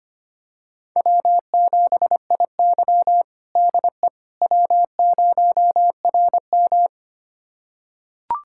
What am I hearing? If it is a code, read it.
Morse code, heard as a keyed tone, about 25 wpm, 700 Hz. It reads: W7IY DE W0RM
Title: USA Callsign DE Callsign - 4 Characters - 25wpm